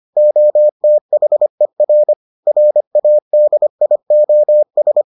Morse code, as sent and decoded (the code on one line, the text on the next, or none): --- - .... .-. / .-. .- -.. .. --- ...
OTHER RADIOS